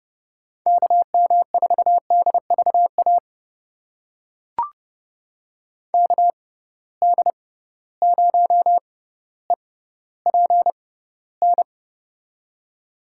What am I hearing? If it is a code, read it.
Morse code, heard as a keyed tone, about 30 wpm, spaced as for 15 wpm, 700 Hz. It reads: KM4DVA KD0EPN
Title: USA Call Signs - 6 Characters - 6x Character Spacing - 30wpm